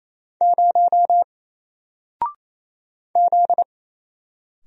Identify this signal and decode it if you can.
Morse code, heard as a keyed tone, about 28 wpm, 700 Hz. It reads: 0 Z
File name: Single-Letter-Number-28wpm